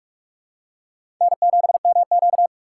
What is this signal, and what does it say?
Morse code, heard as a keyed tone, about 43 wpm, 700 Hz. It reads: N7MQ
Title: USA Call Signs - 4 Characters - 6x Character Spacing - 45wpm